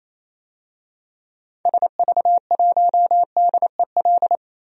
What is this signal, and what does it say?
Morse code, heard as a keyed tone, about 28 wpm, 700 Hz. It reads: SV1DEL
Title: DXCC-100 Call Signs - 6 Characters - Repeated 2x - 28wpm